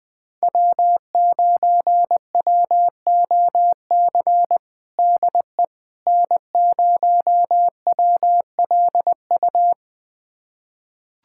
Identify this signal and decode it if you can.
Morse code, heard as a keyed tone, about 21 wpm, 700 Hz. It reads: W9WOC DE N0WLU
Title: USA Callsign DE Callsign - 5 Characters - 20wpm